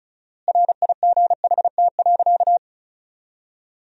Morse code, heard as a keyed tone, about 34 wpm, 700 Hz. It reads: RIGHT.
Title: 4x Word-Spacing - Sentences from Top 500 Words - 35wpm